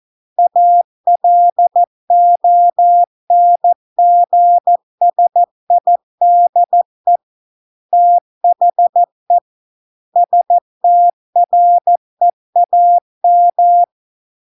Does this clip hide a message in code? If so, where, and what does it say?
Yes: Morse code from 0.4 to 13.8 s, heard as a keyed tone, about 14 wpm, 700 Hz. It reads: ALONGSIDE THE STREAM